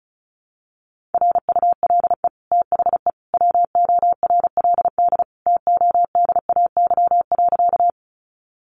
Morse code, heard as a keyed tone, about 34 wpm, 700 Hz. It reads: RULE THE WORLD TODAY.
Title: Sentences from Top 600 Words 35wpm